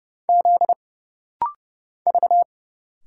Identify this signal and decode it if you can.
Morse code, heard as a keyed tone, about 30 wpm, 700 Hz. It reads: Z V
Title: Single Letters - Rapid-Fire 30wpm